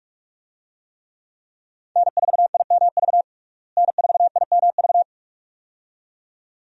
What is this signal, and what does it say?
Morse code, heard as a keyed tone, about 47 wpm, 700 Hz. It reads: N4IMV N4IMV
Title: USA Call Signs - 5 Characters - Repeated 2x - 45wpm